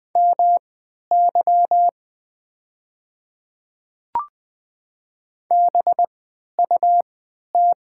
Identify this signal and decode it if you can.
Morse code, heard as a keyed tone, about 20 wpm, spaced as for 11 wpm, 700 Hz. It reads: MY BUT
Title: 3x Character Spacing - Top 100 Words - 20wpm